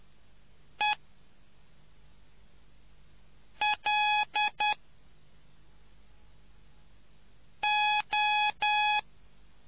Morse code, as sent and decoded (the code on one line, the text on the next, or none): . / .-.. / ---
E L O